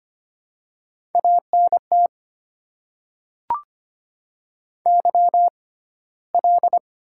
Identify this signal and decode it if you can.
Morse code, heard as a keyed tone, about 25 wpm, spaced as for 15 wpm, 690 Hz. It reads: ANT YL
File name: Top 90 QSO Elements - 6x Character Spacing - 25wpm